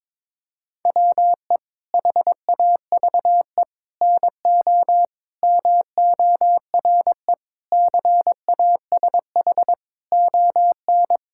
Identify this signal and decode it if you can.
Morse code, heard as a keyed tone, about 22 wpm, 700 Hz. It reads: WE HAVE NO MORE CASH ON
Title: Sentences from Top 1000 Words - Review Part 5 - 22wpm